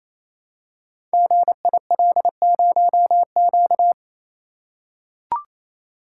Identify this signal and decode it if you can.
Morse code, heard as a keyed tone, about 29 wpm, 700 Hz. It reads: GIL0Q
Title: Code Groups - 5 Letter:Numbers - 28wpm